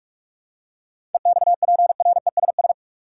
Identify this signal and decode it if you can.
Morse code, heard as a keyed tone, about 42 wpm, 700 Hz. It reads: EXPRESS
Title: Seven Letter Words - 45wpm